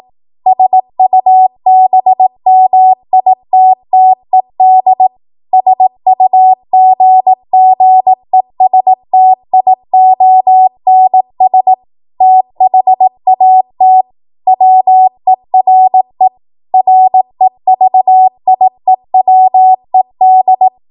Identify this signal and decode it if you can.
Morse code, heard as a keyed tone, about 18 wpm, 750 Hz. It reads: SUBMITTED SUGGESTIONS THAT WERE REVIEWED
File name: Morse / 18WPM